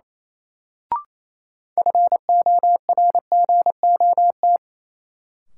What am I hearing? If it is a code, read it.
Morse code, heard as a keyed tone, about 28 wpm, 700 Hz. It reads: FORGOT
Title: Sentences from Top 1400 Words - New Word Drill - 28wpm